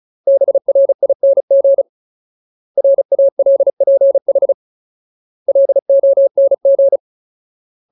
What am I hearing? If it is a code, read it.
Morse code, heard as a keyed tone, about 36 wpm, 550 Hz. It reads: BRING RALPH LONG
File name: Ultimate Word List / 35WPM Ultimate